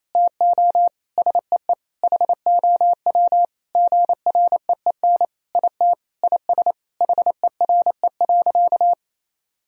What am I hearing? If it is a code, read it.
Morse code, heard as a keyed tone, about 28 wpm, 700 Hz. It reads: TO SEE HOW GREEN IT IS HERE.
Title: Sentences from Top 1000 Words - Review Part 2 - 28wpm